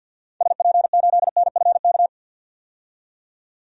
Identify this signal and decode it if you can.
Morse code, heard as a keyed tone, about 52 wpm, 700 Hz. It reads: SP8NFK
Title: DXCC-100 Call Signs - 6 Characters - 50wpm